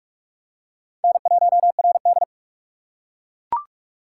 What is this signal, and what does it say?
Morse code, heard as a keyed tone, about 41 wpm, 700 Hz. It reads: N1RD
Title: USA Call Signs - 4 Characters - 4x Character Spacing - 45wpm